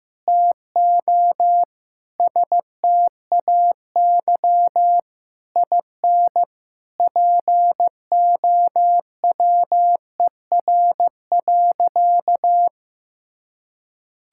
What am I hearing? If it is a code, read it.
Morse code, heard as a keyed tone, about 15 wpm, 700 Hz. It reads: TO STAY IN POWER.
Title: Sentences from Top 1000 Words - Review Part 5 - 15wpm